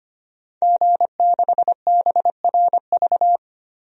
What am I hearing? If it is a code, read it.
Morse code, heard as a keyed tone, about 25 wpm, 700 Hz. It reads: G6BRV
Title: DXCC-100 Call Signs - 5 Characters - 25wpm